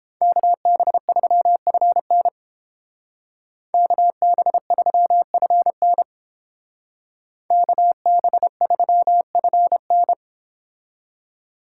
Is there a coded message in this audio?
Yes: Morse code from 0.2 to 10.1 s, heard as a keyed tone, about 30 wpm, 700 Hz. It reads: KB3FN KB3FN KB3FN